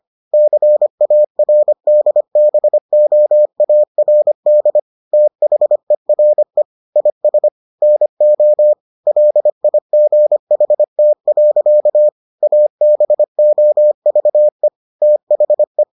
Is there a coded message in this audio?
Yes: Morse code from 0.3 to 15.8 s, heard as a keyed tone, about 25 wpm, 600 Hz. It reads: CARDBOARD THERE IS NO LIGHT. ABOVE THE